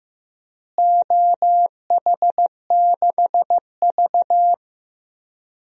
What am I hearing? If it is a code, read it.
Morse code, heard as a keyed tone, about 15 wpm, 700 Hz. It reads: OH6V
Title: DXCC-100 Call Signs - 4 Characters - Repeated 2x - 15wpm